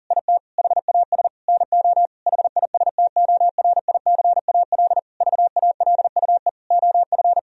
Morse code, heard as a keyed tone, about 40 wpm, 700 Hz. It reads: IT HAS NO HISTORICAL VALUE OF
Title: Sentences from Top 1000 Words - Review Part 4 - 40wpm